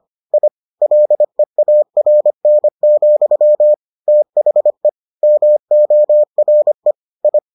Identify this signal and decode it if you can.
Morse code, heard as a keyed tone, about 25 wpm, 600 Hz. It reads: I LEARN, THE MORE I